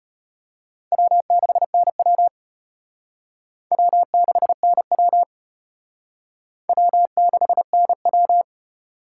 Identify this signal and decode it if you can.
Morse code, heard as a keyed tone, about 31 wpm, 700 Hz. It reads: W6NW W6NW W6NW
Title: Speed Racing - USA 1x2 Call Signs 25wpm